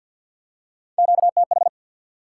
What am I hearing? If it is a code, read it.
Morse code, heard as a keyed tone, about 53 wpm, 700 Hz. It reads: QTH
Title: Top 90 QSO Elements 50wpm